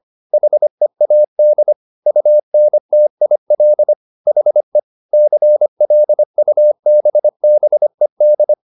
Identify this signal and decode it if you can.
Morse code, heard as a keyed tone, about 25 wpm, 600 Hz. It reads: HEAD UNTIL HE CLUBBED